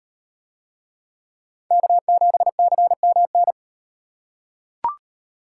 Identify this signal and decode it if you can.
Morse code, heard as a keyed tone, about 36 wpm, 700 Hz. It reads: K7CMN